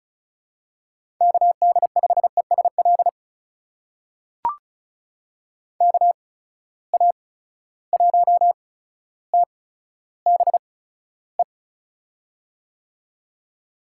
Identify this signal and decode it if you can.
Morse code, heard as a keyed tone, about 36 wpm, spaced as for 15 wpm, 700 Hz. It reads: KD5ESL KA1TBE
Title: USA Call Signs - 6 Characters - 8x Character Spacing - 35wpm